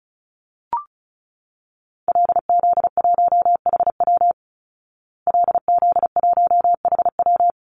Morse code, heard as a keyed tone, about 34 wpm, 700 Hz. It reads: LZ1HW LZ1HW
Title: Call Signs Repeated 3x - 35wpm